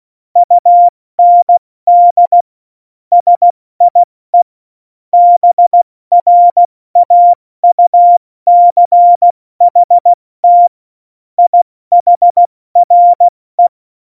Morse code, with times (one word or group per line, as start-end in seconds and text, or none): 0.4-2.4 UND
3.1-4.4 SIE
5.1-10.7 BRAUCHT
11.4-13.7 IHRE